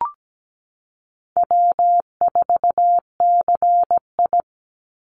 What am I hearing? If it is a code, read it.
Morse code, heard as a keyed tone, about 17 wpm, 700 Hz. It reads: W4CI
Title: Easy Call Signs Repeated 3x - 17wpm